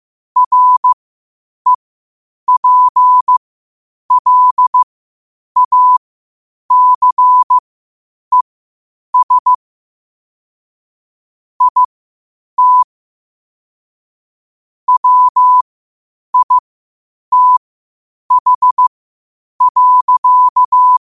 Morse code, heard as a keyed tone, about 15 wpm, spaced as for 8 wpm, 1000 Hz. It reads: REPLACES IT WITH.
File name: qotdfiles / 8wpm